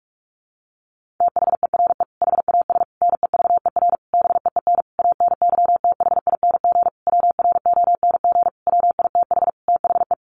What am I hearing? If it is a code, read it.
Morse code, heard as a keyed tone, about 44 wpm, 700 Hz. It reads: THERE HAS NEVER BEEN ANYTHING WRONG WITH THE
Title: Sentences from Top 1500 Words 45wpm